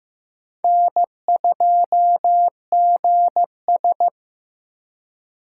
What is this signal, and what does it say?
Morse code, heard as a keyed tone, about 15 wpm, 700 Hz. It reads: N2GS
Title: USA Call Signs - 3x Character Spacing - 15wpm